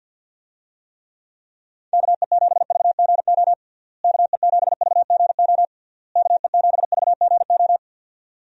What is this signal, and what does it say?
Morse code, heard as a keyed tone, about 50 wpm, 700 Hz. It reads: KE7VGO KE7VGO KE7VGO
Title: USA Call Signs - 6 Characters - Repeated 3x - 50wpm